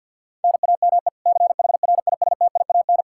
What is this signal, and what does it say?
Morse code, heard as a keyed tone, about 49 wpm, 700 Hz. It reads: NAME CHRISTIAN
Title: K1USN SST QSO Practice 50wpm